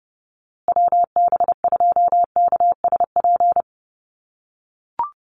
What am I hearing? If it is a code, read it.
Morse code, heard as a keyed tone, about 30 wpm, 700 Hz. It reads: WB2KSP
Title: USA Call Signs Repeated 3x 30wpm